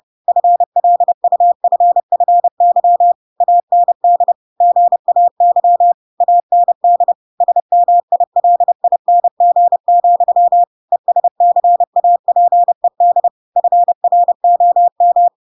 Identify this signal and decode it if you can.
Morse code, heard as a keyed tone, about 30 wpm, 700 Hz. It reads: FLUFFY AND GAY AND SMILING, ESCAPED FROM
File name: Pinocchio / Pinocchio ch5 30 30 ik1hge